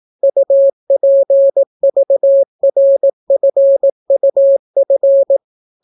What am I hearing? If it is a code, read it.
Morse code, heard as a keyed tone, about 18 wpm, 550 Hz. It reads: UPVRFUF